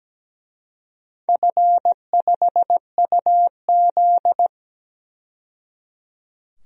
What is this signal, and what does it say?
Morse code, heard as a keyed tone, about 17 wpm, 700 Hz. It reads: F5UZ
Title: DXCC-100 Call Signs - 4 Characters - 17wpm